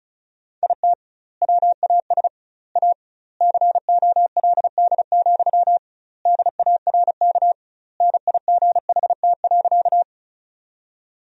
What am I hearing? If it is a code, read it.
Morse code, heard as a keyed tone, about 36 wpm, 700 Hz. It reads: IT WAS A COLD, DARK NIGHT.
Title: 2x Word-Spacing - Sentences from Top 500 Words - 35wpm